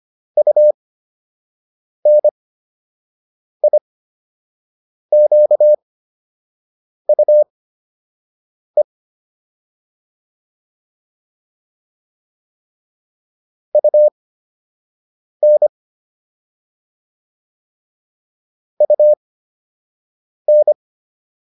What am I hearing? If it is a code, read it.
Morse code, heard as a keyed tone, about 25 wpm, spaced as for 6 wpm, 600 Hz. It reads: UNIQUE UN UN